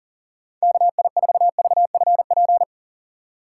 Morse code, heard as a keyed tone, about 38 wpm, 700 Hz. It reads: KI4VFP
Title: USA Call Signs - 6 Characters - 2x Character Spacing - 40wpm